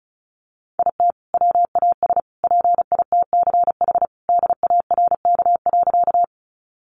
Morse code, heard as a keyed tone, about 34 wpm, 700 Hz. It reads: IT WAS PITCH DARK.
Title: Sentences from Top 1300 Words 35wpm